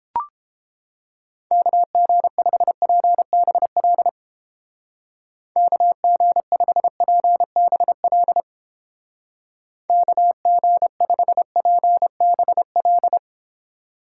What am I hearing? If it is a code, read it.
Morse code, heard as a keyed tone, about 31 wpm, 700 Hz. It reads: KG5PBL KG5PBL KG5PBL